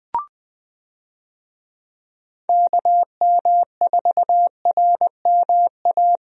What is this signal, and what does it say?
Morse code, heard as a keyed tone, about 20 wpm, 700 Hz. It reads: KM4RMA
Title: USA Call Signs - 6 Characters - Repeated 2x - 20wpm